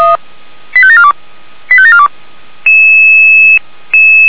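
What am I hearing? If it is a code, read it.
Five-tone selective call (ZVEI), heard as quick runs of five tones, tones of about 70 ms. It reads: 85742 85742
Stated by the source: ZVEI five-tone selective call received on an SDR.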